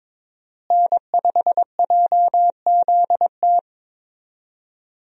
Morse code, heard as a keyed tone, about 22 wpm, 700 Hz. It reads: N5JZT
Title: USA Call Signs - 5 Characters - Repeated 2x - 22wpm